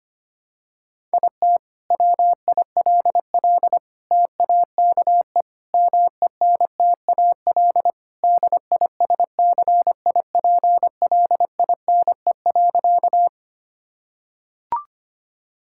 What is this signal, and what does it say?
Morse code, heard as a keyed tone, about 25 wpm, 700 Hz. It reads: IT WILL TAKE MENTAL DISCIPLINE.
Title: Sentences from Top 1000 Words - Review Part 6 - 25wpm